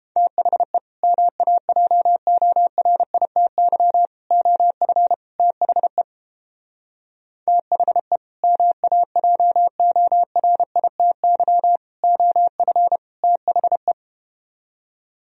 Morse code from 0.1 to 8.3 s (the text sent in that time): THE MAJORITY OF THE THE